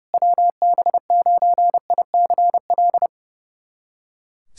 Morse code, heard as a keyed tone, about 30 wpm, 700 Hz. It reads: WB9ICL
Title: SOTA QSO Practice 30wpm